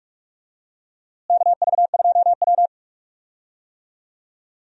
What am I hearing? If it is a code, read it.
Morse code, heard as a keyed tone, about 46 wpm, 700 Hz. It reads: KV2W
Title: USA Call Signs - 4 Characters - Repeated 2x - 45wpm